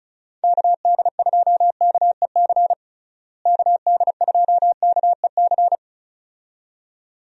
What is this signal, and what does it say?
Morse code, heard as a keyed tone, about 36 wpm, 700 Hz. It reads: KD2KEC KD2KEC
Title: USA Call Signs - 6 Characters - Repeated 2x - 35wpm